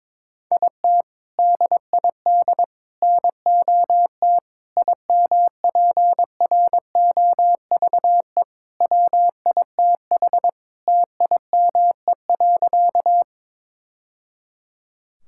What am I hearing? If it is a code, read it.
Morse code, heard as a keyed tone, about 22 wpm, 700 Hz. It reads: IT DID NOT IMPROVE WITH TIME.
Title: Sentences from Top 1000 Words - Review Part 5 - 22wpm